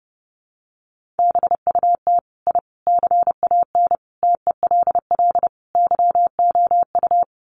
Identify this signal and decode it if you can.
Morse code, heard as a keyed tone, about 30 wpm, 700 Hz. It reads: BUT I CAN TELL YOU
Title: Sets of 5 Words Encore 30wpm